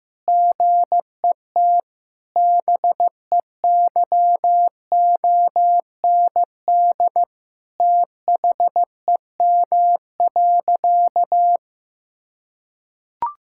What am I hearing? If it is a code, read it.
Morse code, heard as a keyed tone, about 16 wpm, 700 Hz. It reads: GET BEYOND THEM.
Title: Sentences from Top 1000 Words - Review Part 6 - 15wpm